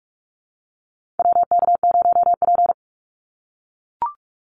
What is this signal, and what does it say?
Morse code, heard as a keyed tone, about 44 wpm, 700 Hz. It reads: WK0P
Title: Call Signs - 45wpm